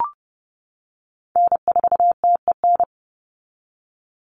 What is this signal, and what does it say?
Morse code, heard as a keyed tone, about 30 wpm, 700 Hz. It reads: N4TEN
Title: USA Call Signs 30wpm